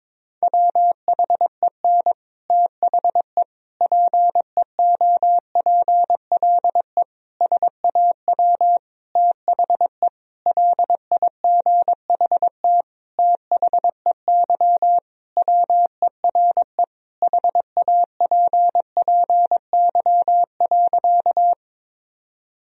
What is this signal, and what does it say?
Morse code, heard as a keyed tone, about 23 wpm, 700 Hz. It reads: WHEN THE PEOPLE SAW THE LIGHT THEY WERE HAPPY.
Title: Sentences from Top 1000 Words - Review Part 1 - 22wpm